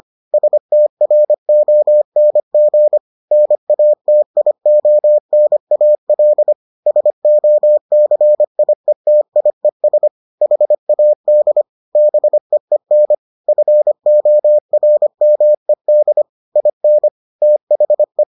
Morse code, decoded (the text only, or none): STRONG NATIONAL SOCIETIES HAD BEEN FORMED IN THE